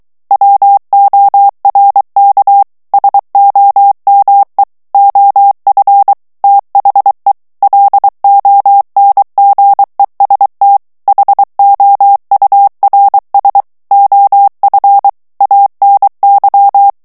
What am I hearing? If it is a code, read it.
Morse code, heard as a keyed tone, about 23 wpm, 800 Hz. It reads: WORK SOME OF THE LONGEST HOURS OF ANY
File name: morse mp3 / ANGLAIS texte clair 1 vitesse 22 mots minute